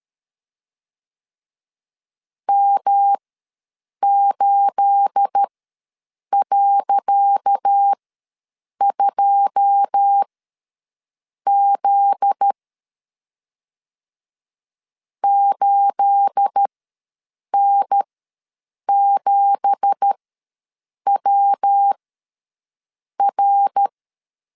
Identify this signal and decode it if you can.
Morse code, heard as a keyed tone, about 12 wpm, spaced as for 7 wpm, 800 Hz. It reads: M8.2Z 8N7WR
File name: LEZIONI / 10 SIGNS / 10C DOT ws 12wpm 3548